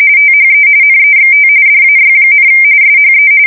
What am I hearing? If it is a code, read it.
RTTY, about 45.45 baud. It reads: THE NEXT DAY, SHE
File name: fsk170 / battleshipnj